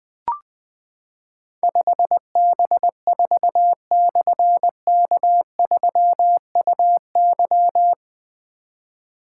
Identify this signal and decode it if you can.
Morse code, heard as a keyed tone, about 20 wpm, 700 Hz. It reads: 5B4/K3UY
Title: Difficult Call Signs 20wpm